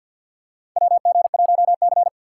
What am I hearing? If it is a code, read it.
Morse code, heard as a keyed tone, about 50 wpm, 700 Hz. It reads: WG1C
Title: USA Call Signs - 4 Characters - 6x Character Spacing - 50wpm